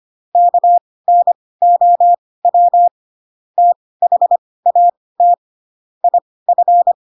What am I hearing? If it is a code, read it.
Morse code, heard as a keyed tone, about 25 wpm, spaced as for 18 wpm, 700 Hz. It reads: KNOW THAT IF
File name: Pinocchio / Pinocchio ch4 12 25 ik1hge